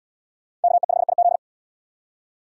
Morse code, heard as a keyed tone, about 77 wpm, 700 Hz. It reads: KE5IZ